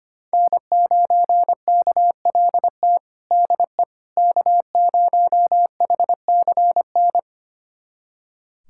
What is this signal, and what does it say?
Morse code, heard as a keyed tone, about 25 wpm, 700 Hz. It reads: N9KLT DE K0HCN
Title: USA Callsign DE Callsign - 5 Characters - 25wpm